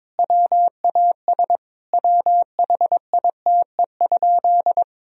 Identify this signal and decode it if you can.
Morse code, heard as a keyed tone, about 22 wpm, 700 Hz. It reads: WAS WHITE?
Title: Sentences from Top 1000 Words - Review Part 2 - 22wpm